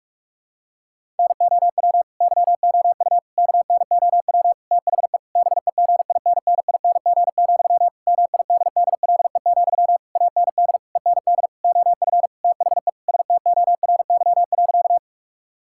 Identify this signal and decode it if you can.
Morse code, heard as a keyed tone, about 46 wpm, 700 Hz. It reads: NOW YOU KNOW THE BEGINNING, MIDDLE, AND END OF THE STORY.
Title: Sentences from Top 1000 Words - Review Part 6 - 45wpm